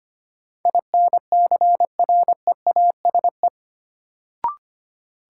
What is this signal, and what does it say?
Morse code, heard as a keyed tone, about 25 wpm, 700 Hz. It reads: INCREASE
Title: Sets of 5 Words - New Word Drill - 25wpm